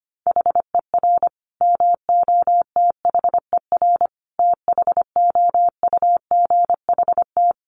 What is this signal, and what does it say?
Morse code, heard as a keyed tone, about 25 wpm, 700 Hz. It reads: HER MOTHER THOUGHT
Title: Sentences from Top 800 Words 25wpm